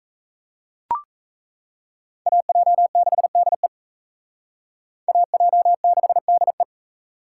Morse code, heard as a keyed tone, about 34 wpm, 700 Hz. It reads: AJ6DE AJ6DE
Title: Speed Racing - USA Call Signs 28wpm